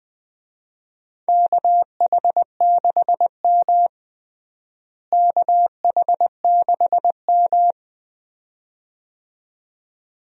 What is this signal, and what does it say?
Morse code, heard as a keyed tone, about 20 wpm, 700 Hz. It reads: KH6M KH6M
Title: DXCC-100 Call Signs - 4 Characters - Repeated 2x - 20wpm